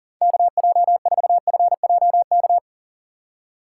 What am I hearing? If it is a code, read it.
Morse code, heard as a keyed tone, about 40 wpm, 700 Hz. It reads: KJ4FJK
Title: USA Call Signs - 6 Characters - 6x Character Spacing - 40wpm